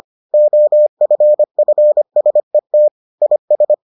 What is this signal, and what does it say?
Morse code, heard as a keyed tone, about 25 wpm, 600 Hz. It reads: OFFSET IS